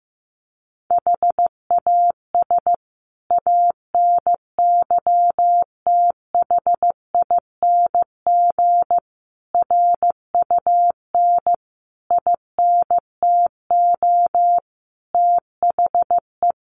Morse code, heard as a keyed tone, about 15 wpm, 700 Hz. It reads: HAS ANYTHING RUN INTO THE
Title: Sentences from Top 1400 Words 15wpm